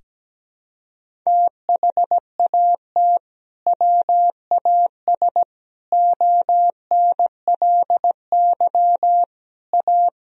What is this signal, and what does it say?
Morse code, heard as a keyed tone, about 17 wpm, 700 Hz. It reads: THAT WAS ONLY A
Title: Sentences from Top 1000 Words - Review Part 1 - 17wpm